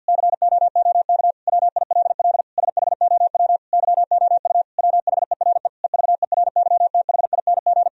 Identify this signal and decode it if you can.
Morse code, heard as a keyed tone, about 50 wpm, 700 Hz. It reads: COOK WILL SHOW YOU WHERE EVERYTHING
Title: Sentences from Top 1000 Words - Review Part 4 - 50wpm